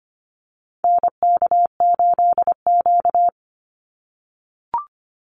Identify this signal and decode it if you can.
Morse code, heard as a keyed tone, about 25 wpm, 700 Hz. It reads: NK8Q
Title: NA SOTA Activator Callsigns 25wpm